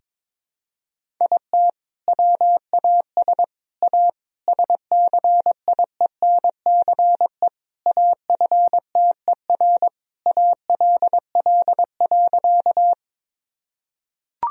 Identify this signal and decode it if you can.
Morse code, heard as a keyed tone, about 22 wpm, 700 Hz. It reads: IT WAS A SCIENCE AFTER ALL.
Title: Sentences from Top 1000 Words - Review Part 2 - 22wpm